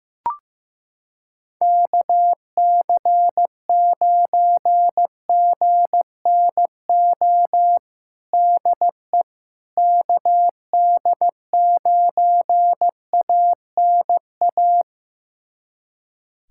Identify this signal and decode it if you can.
Morse code, heard as a keyed tone, about 15 wpm, 700 Hz. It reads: KC9GNO DE KD9ANA
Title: USA Callsign DE Callsign - 6 Characters - 15wpm